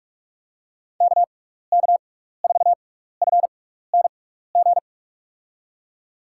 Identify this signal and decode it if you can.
Morse code, heard as a keyed tone, about 44 wpm, spaced as for 15 wpm, 700 Hz. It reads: KK4FNG
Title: USA Call Signs - 6 Characters - 6x Character Spacing - 45wpm